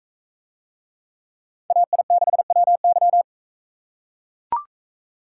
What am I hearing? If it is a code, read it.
Morse code, heard as a keyed tone, about 40 wpm, 700 Hz. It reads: AI6WY